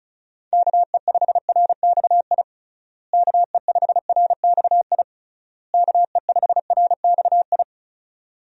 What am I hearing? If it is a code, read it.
Morse code, heard as a keyed tone, about 36 wpm, 700 Hz. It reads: KE5RXI KE5RXI KE5RXI
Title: USA Call Signs - 6 Characters - Repeated 3x - 35wpm